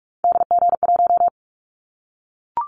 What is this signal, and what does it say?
Morse code, heard as a keyed tone, about 44 wpm, 700 Hz. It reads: DG1
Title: Callsign Prefixes for Top 25 DXCC - 45wpm